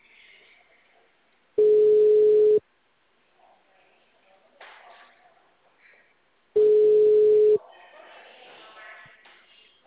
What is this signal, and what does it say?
Ringback tone (Europe, 425 Hz), with speech.